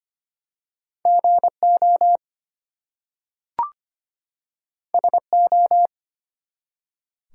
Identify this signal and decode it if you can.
Morse code, heard as a keyed tone, about 25 wpm, 700 Hz. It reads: GO SO
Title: Two Letter Words - 25wpm